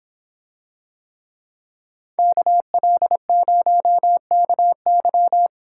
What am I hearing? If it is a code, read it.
Morse code, heard as a keyed tone, about 26 wpm, 700 Hz. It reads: KL0KY